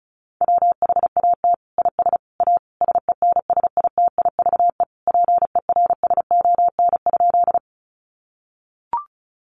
Morse code, heard as a keyed tone, about 34 wpm, 700 Hz. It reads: WHAT IS A SENSITIVE PERSON?
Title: Sentences from Top 1400 Words 35wpm